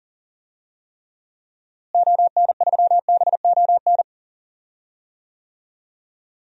Morse code, heard as a keyed tone, about 40 wpm, 700 Hz. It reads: ON3BON
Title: DXCC-100 Call Signs - 6 Characters - Repeated 2x - 40wpm